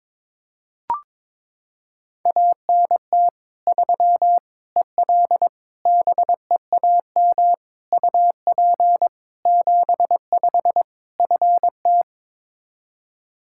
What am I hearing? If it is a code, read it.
Morse code, heard as a keyed tone, about 22 wpm, 700 Hz. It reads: ANT 3 EL BEAM UP 75 FT
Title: Practice QSO - Exchange 2 - Info - 22wpm